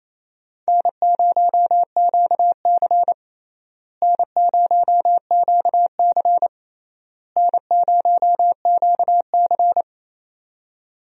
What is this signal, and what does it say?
Morse code, heard as a keyed tone, about 28 wpm, 700 Hz. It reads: N0QC N0QC N0QC
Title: USA Call Signs - 4 Characters - Repeated 3x - 28wpm